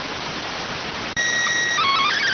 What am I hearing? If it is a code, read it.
SSTV (slow-scan television), PD 290.